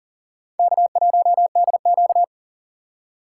Morse code, heard as a keyed tone, about 40 wpm, 700 Hz. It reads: K1DQ
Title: USA Call Signs - 4 Characters - 3x Character Spacing - 40wpm